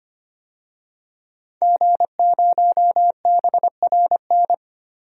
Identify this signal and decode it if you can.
Morse code, heard as a keyed tone, about 25 wpm, 700 Hz. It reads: G0BRN
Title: DXCC-100 Call Signs - 5 Characters - 25wpm